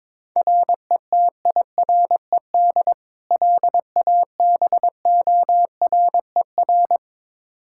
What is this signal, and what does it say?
Morse code, heard as a keyed tone, about 22 wpm, 700 Hz. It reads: RETIRED LABORER
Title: Practice QSO - Exchange 3 - More Info - 22wpm